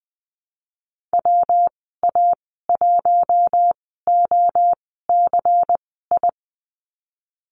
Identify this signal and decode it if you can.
Morse code, heard as a keyed tone, about 20 wpm, spaced as for 14 wpm, 700 Hz. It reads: WA1OCI
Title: 122 Lesson 20g - Number 3 - Call Signs 20wpm